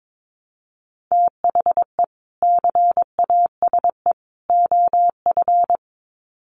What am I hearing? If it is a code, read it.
Morse code, heard as a keyed tone, about 22 wpm, 700 Hz. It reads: THE CASE OF
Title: Sets of 3 Words 22wpm